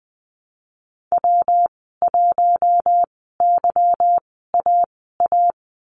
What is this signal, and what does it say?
Morse code, heard as a keyed tone, about 20 wpm, spaced as for 15 wpm, 700 Hz. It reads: W1YAA
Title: 147 Lesson 24g - Letter y - Call Signs 20wpm